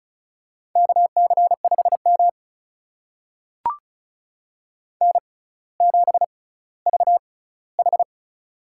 Morse code, heard as a keyed tone, about 36 wpm, spaced as for 18 wpm, 700 Hz. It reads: KC5M N7VH
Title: USA Call Signs - 4 Characters - 6x Character Spacing - 35wpm